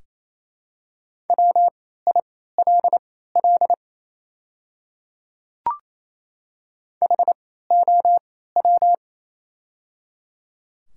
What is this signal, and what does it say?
Morse code, heard as a keyed tone, about 28 wpm, spaced as for 16 wpm, 700 Hz. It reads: WILL HOW
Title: 3x Character Spacing - Top 100 Words - 28wpm